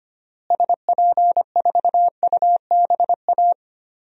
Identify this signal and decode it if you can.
Morse code, heard as a keyed tone, about 25 wpm, 700 Hz. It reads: SP4UBA
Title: DXCC-100 Call Signs - 6 Characters - Repeated 2x - 25wpm